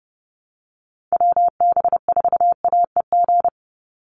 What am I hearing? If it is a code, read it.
Morse code, heard as a keyed tone, about 30 wpm, 700 Hz. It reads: WB4AEG
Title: Call Signs - 30wpm